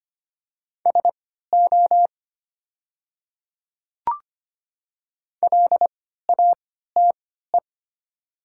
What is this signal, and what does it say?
Morse code, heard as a keyed tone, about 25 wpm, spaced as for 14 wpm, 700 Hz. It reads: SO LATE
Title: 3x Character Spacing - Top 300 Words - 25wpm